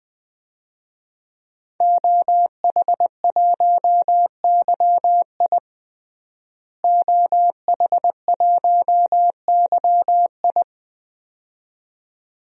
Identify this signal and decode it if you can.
Morse code, heard as a keyed tone, about 20 wpm, 700 Hz. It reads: OH1YI OH1YI
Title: DXCC-100 Call Signs - 5 Characters - Repeated 2x - 20wpm